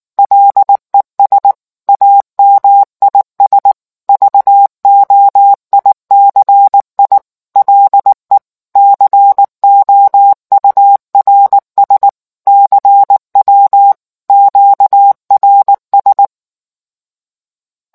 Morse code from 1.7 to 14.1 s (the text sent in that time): AMIS VOICI LE COURS CW